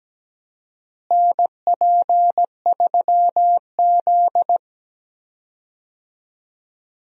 Morse code, heard as a keyed tone, about 17 wpm, 700 Hz. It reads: NP3Z
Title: USA Call Signs - 4 Characters - Repeated 3x - 17wpm